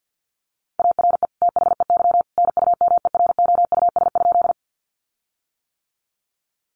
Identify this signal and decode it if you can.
Morse code, heard as a keyed tone, about 50 wpm, 700 Hz. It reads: ARE THEY NUMEROUS?
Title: Sentences from Top 1400 Words 50wpm